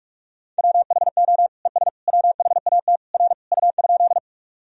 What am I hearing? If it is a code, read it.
Morse code, heard as a keyed tone, about 41 wpm, 700 Hz. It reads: WHO ES WHAT R U?
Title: Sentences from Top 100 Words and Abbreviations 45wpm